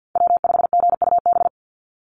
Morse code, heard as a keyed tone, about 50 wpm, 700 Hz. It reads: W5GUB
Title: USA Call Signs 50wpm